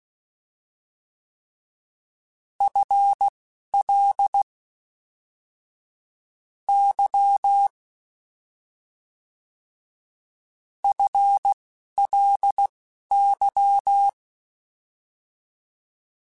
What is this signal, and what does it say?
Morse code, heard as a keyed tone, about 16 wpm, spaced as for 11 wpm, 800 Hz. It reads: FL Y FLY
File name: FL / FL mp3s / Fast FL @10